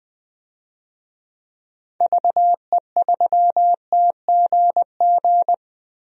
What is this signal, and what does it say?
Morse code, heard as a keyed tone, about 20 wpm, 700 Hz. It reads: VE3TGG
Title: DXCC-100 Call Signs - 6 Characters - Repeated 3x - 20wpm